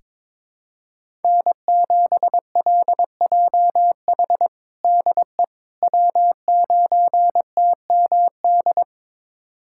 Morse code, heard as a keyed tone, about 21 wpm, 700 Hz. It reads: N7LJH DE W9TMD
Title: USA Callsign DE Callsign - 5 Characters - 22wpm